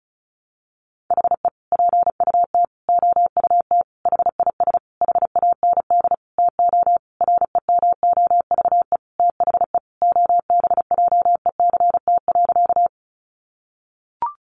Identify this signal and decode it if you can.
Morse code, heard as a keyed tone, about 34 wpm, 700 Hz. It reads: HE PUT OUT HIS HAND TO REMOVE THE OBJECT.
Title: Sentences from Top 800 Words 35wpm